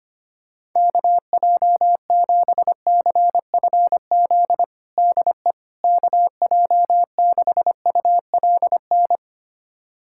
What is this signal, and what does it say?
Morse code, heard as a keyed tone, about 25 wpm, 700 Hz. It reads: KJ7CFZ DE KJ6ULN
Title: USA Callsign DE Callsign - 6 Characters - 25wpm